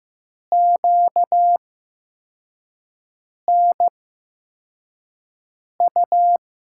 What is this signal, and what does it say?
Morse code, heard as a keyed tone, about 15 wpm, spaced as for 4 wpm, 700 Hz. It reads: QNU
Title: USA Call Signs - 8x Character Spacing - 15wpm